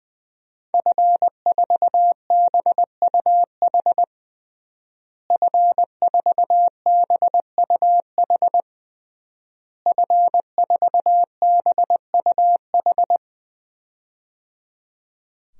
Morse code, heard as a keyed tone, about 20 wpm, 700 Hz. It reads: F4BUH F4BUH F4BUH
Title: DXCC-100 Call Signs - 5 Characters - Repeated 3x - 20wpm